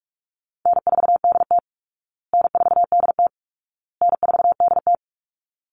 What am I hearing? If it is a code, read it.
Morse code, heard as a keyed tone, about 44 wpm, 700 Hz. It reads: N4DT N4DT N4DT
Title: Call Signs Repeated 2x - 45wpm